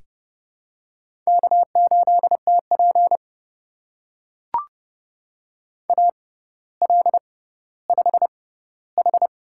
Morse code, heard as a keyed tone, about 30 wpm, spaced as for 15 wpm, 700 Hz. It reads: K8TP AL5H
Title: USA Call Signs - 6x Character Spacing - 30wpm